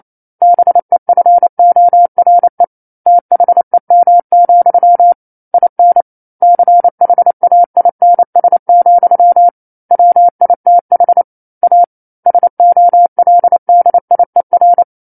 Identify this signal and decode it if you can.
Morse code, heard as a keyed tone, about 28 wpm, 700 Hz. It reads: BEFORE THEM, IN CHAINS, WITH A SOLDIER
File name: 09-Chapter-09 28 WPM